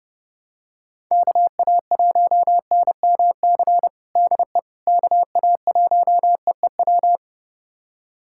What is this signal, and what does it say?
Morse code, heard as a keyed tone, about 32 wpm, 700 Hz. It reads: KA1NMC DE KA1EEW
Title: USA Callsign DE Callsign 30wpm